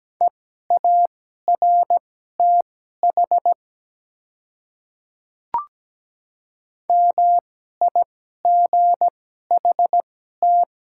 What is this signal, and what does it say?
Morse code, heard as a keyed tone, about 17 wpm, spaced as for 12 wpm, 700 Hz. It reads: EARTH MIGHT
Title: 2x Character Spacing - Top 300 Words - 17wpm